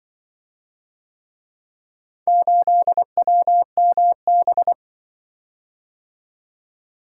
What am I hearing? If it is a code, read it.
Morse code, heard as a keyed tone, about 24 wpm, 700 Hz. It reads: 8WMB
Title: Speed-Racing - Four Letter-Number Combinations - 20wpm